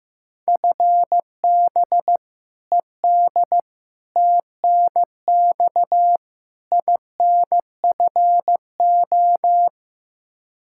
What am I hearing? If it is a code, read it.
Morse code, heard as a keyed tone, about 15 wpm, 700 Hz. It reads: FB ED TNX INFO
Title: Practice QSO - Exchange 3 - More Info - 15wpm